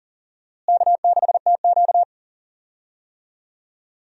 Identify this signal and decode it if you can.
Morse code, heard as a keyed tone, about 38 wpm, 700 Hz. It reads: K6TQ
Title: USA Call Signs - 4 Characters - Repeated 3x - 40wpm